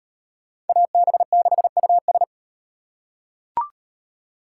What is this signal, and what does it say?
Morse code, heard as a keyed tone, about 38 wpm, 700 Hz. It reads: AB6US